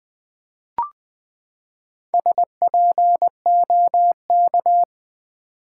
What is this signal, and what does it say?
Morse code, heard as a keyed tone, about 20 wpm, 700 Hz. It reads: SPOK
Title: Build A Word - 4 Letters At A Time - 11-Letter Words - 20wpm